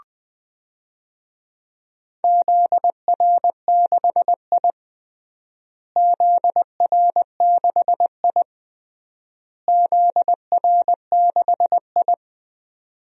Morse code, heard as a keyed tone, about 20 wpm, 700 Hz. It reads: ZR6I ZR6I ZR6I
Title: DXCC-100 Call Signs - 4 Characters - Repeated 3x - 20wpm